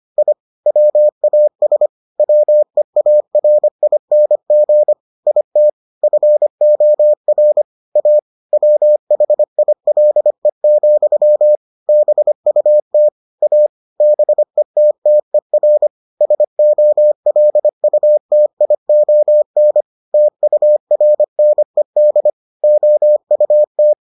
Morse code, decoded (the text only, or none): I WAS WEARING IT FOR A WHILE, BUT A BETTER SOLUTION TURNED OUT